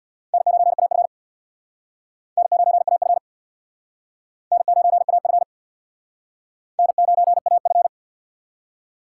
Morse code, heard as a keyed tone, about 60 wpm, 700 Hz. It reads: N9AF N9AF N9AF N9AF
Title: Speed Racing - USA 1x2 Call Signs 50wpm